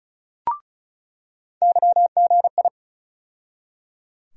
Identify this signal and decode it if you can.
Morse code, heard as a keyed tone, about 36 wpm, 700 Hz. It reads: YGI
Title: USA Call Sign Postfixes - 3 Letters 35wpm